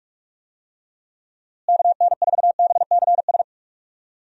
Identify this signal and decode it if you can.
Morse code, heard as a keyed tone, about 47 wpm, 700 Hz. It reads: KN4BCS
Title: USA Call Signs - 6 Characters - Repeated 2x - 45wpm